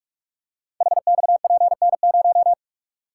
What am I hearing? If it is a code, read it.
Morse code, heard as a keyed tone, about 44 wpm, 700 Hz. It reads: HXPN0
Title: Code Groups - 5 Letter:Numbers - 45wpm